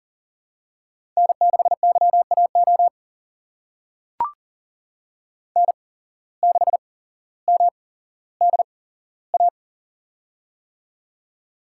Morse code, heard as a keyed tone, about 39 wpm, spaced as for 17 wpm, 700 Hz. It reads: N6YAO N6MDA
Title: USA Call Signs - 5 Characters - 8x Character Spacing - 40wpm